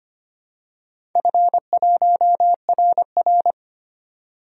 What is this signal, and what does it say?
Morse code, heard as a keyed tone, about 25 wpm, 700 Hz. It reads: F1RR
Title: DXCC-100 Call Signs - 4 Characters - 25wpm